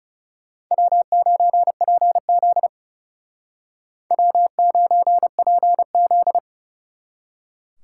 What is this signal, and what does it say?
Morse code, heard as a keyed tone, about 38 wpm, 700 Hz. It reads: W9PZ W9PZ